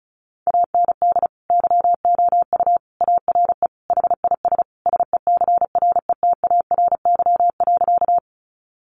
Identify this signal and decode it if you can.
Morse code, heard as a keyed tone, about 34 wpm, 700 Hz. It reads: AND YOU ARE HIS SECRETARY.
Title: Sentences from Top 1300 Words 35wpm